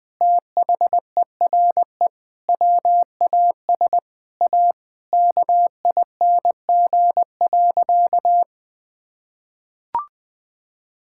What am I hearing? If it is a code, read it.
Morse code, heard as a keyed tone, about 20 wpm, 700 Hz. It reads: THERE WAS A KING.
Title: Sentences with 7-Word Phrases - 20wpm